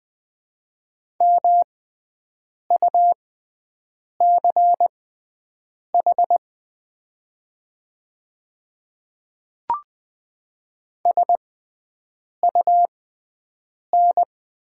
Morse code, heard as a keyed tone, about 20 wpm, spaced as for 7 wpm, 700 Hz. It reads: MUCH SUN
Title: 6x Character Spacing - Top 300 Words - 20wpm